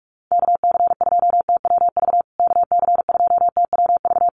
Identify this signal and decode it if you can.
Morse code, heard as a keyed tone, about 44 wpm, 700 Hz. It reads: KC2TWV KC2TWV
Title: USA Call Signs Repeated 3x 45wpm